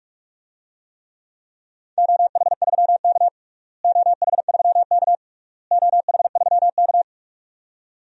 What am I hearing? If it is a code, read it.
Morse code, heard as a keyed tone, about 46 wpm, 700 Hz. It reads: OH3K OH3K OH3K
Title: DXCC-100 Call Signs - 4 Characters - Repeated 3x - 45wpm